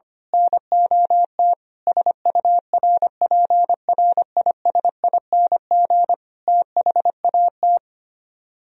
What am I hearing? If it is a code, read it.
Morse code, heard as a keyed tone, about 24 wpm, 700 Hz. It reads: NOT SURPRISING THAT
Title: Speed-Racing - Sets of 5 Words 25wpm